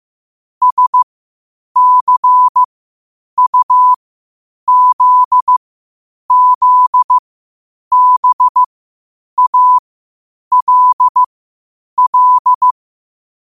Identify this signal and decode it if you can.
Morse code, heard as a keyed tone, about 15 wpm, spaced as for 8 wpm, 1000 Hz. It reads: SCUZZBALL
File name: qotdfiles / 8wpm